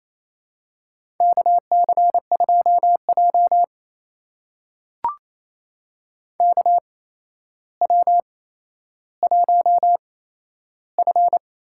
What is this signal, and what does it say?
Morse code, heard as a keyed tone, about 28 wpm, spaced as for 12 wpm, 700 Hz. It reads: KC2J KW1F
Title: USA Call Signs - 4 Characters - 8x Character Spacing - 28wpm